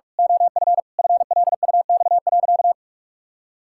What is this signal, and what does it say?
Morse code, heard as a keyed tone, about 46 wpm, 700 Hz. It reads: OF FLUX.
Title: Pangram Challenge v2 - 45wpm